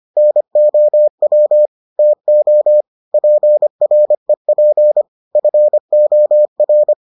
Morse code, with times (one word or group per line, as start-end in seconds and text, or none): 0.2-1.7 NOW
2.0-2.8 TO
3.1-5.0 PREP
5.4-6.9 FOR